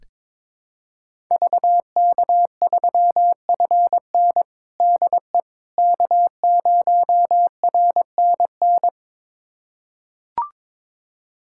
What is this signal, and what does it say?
Morse code, heard as a keyed tone, about 21 wpm, 700 Hz. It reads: VK3FN DE K0RNN
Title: USA Callsign DE Callsign - 5 Characters - 22wpm